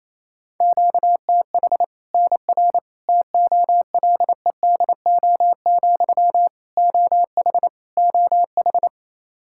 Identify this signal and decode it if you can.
Morse code, heard as a keyed tone, about 28 wpm, 700 Hz. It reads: QTH NR TOLEDO, OH OH
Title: Practice QSO - Exchange 1 - Beginning - 28wpm